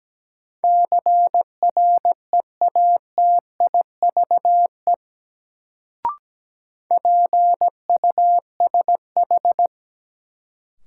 Morse code, heard as a keyed tone, about 17 wpm, 700 Hz. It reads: CREATIVE PUSH
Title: Sentences from Top 1300 Words - New Word Drill Alt - 17wpm